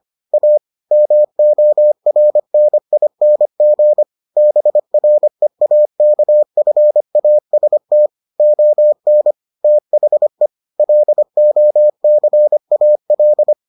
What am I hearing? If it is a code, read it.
Morse code, heard as a keyed tone, about 25 wpm, 600 Hz. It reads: A MORNING BREAKFAST ON THE LOCAL